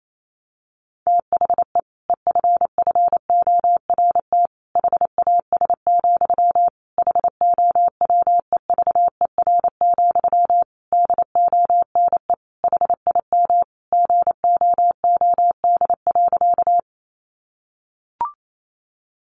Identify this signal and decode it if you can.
Morse code, heard as a keyed tone, about 28 wpm, 700 Hz. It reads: THE EFFORT HAS, HOWEVER, DONE HIM GOOD.
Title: Sentences from Top 1200 Words 28wpm